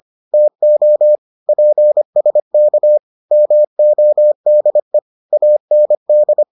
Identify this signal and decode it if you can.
Morse code, heard as a keyed tone, about 25 wpm, 600 Hz. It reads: TO PSK MODE AND